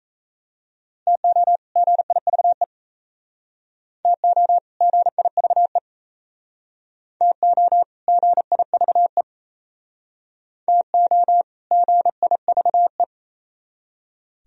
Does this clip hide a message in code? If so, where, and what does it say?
Yes: Morse code from 1.1 to 13.0 s, heard as a keyed tone, about 35 wpm, 700 Hz. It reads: TO GIVE TO GIVE TO GIVE TO GIVE